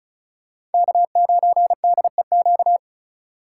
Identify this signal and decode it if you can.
Morse code, heard as a keyed tone, about 36 wpm, 700 Hz. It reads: K9DEQ
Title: USA Call Signs - 5 Characters - 2x Character Spacing - 35wpm